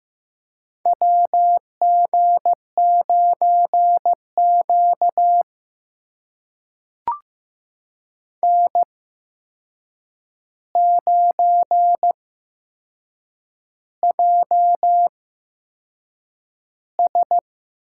Morse code, heard as a keyed tone, about 15 wpm, spaced as for 6 wpm, 700 Hz. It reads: WG9Q N9JS
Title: USA Call Signs - 4 Characters - 8x Character Spacing - 15wpm